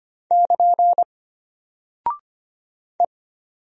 Morse code, heard as a keyed tone, about 25 wpm, 700 Hz. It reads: <KN> E
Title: Single Character (Letter:Number:Punctuation:Prosign) Alt 25wpm